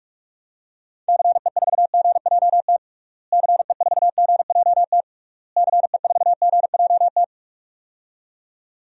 Morse code, heard as a keyed tone, about 46 wpm, 700 Hz. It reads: CE4GJT CE4GJT CE4GJT
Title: DXCC-100 Call Signs - 6 Characters - Repeated 3x - 45wpm